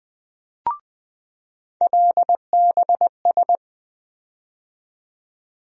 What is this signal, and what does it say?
Morse code, heard as a keyed tone, about 20 wpm, 700 Hz. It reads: LBS
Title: USA Call Sign Postfixes - 3 Letters 20wpm